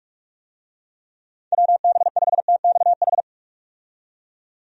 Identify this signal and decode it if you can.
Morse code, heard as a keyed tone, about 44 wpm, 700 Hz. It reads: WB5TXH
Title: USA Call Signs - 6 Characters - 45wpm